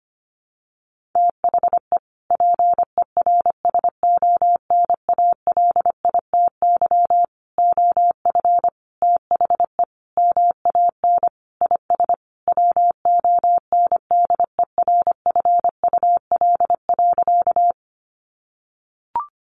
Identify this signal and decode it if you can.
Morse code, heard as a keyed tone, about 25 wpm, 700 Hz. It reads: THE PERSONALITY OF THE MAN IS WONDERFUL.
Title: Sentences from Top 1400 Words 25wpm